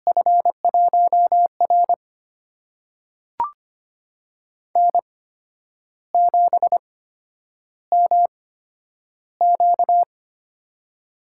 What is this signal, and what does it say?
Morse code, heard as a keyed tone, about 25 wpm, spaced as for 10 wpm, 700 Hz. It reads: F1R N7MQ
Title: USA Call Signs - 4 Characters - 8x Character Spacing - 25wpm